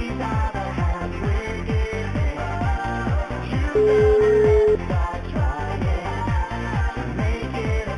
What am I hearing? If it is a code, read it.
Ringback tone (Europe, 425 Hz), with noise.